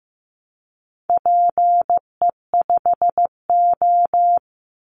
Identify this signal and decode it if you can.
Morse code, heard as a keyed tone, about 15 wpm, 700 Hz. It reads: PE5O
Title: Call Signs Repeated 3x - 15wpm